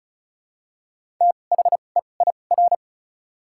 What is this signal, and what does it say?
Morse code, heard as a keyed tone, about 34 wpm, spaced as for 25 wpm, 700 Hz. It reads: THEIR
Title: 2x Character Spacing - Top 200 Words - 35wpm